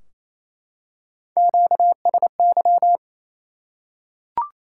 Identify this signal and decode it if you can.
Morse code, heard as a keyed tone, about 27 wpm, 700 Hz. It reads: QSY
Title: Top 180 QSO Elements 28wpm